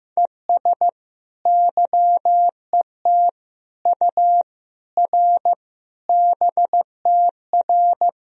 Morse code, heard as a keyed tone, about 15 wpm, 700 Hz. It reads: ES YET U R BTR
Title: Sentences from Top 500 Words and Abbreviations 15wpm